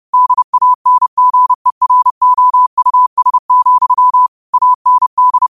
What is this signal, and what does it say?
Morse code, heard as a keyed tone, about 30 wpm, 1000 Hz. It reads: DANGEROUS, AND
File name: qotdfiles / 30wpm